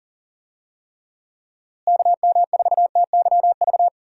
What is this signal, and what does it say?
Morse code, heard as a keyed tone, about 40 wpm, 700 Hz. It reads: KM4TYV
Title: USA Call Signs - 6 Characters - Repeated 3x - 40wpm